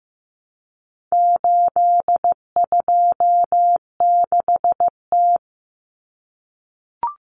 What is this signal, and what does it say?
Morse code, heard as a keyed tone, about 15 wpm, 700 Hz. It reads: 826T
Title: Common Cut Numbers - 4-Digit 15wpm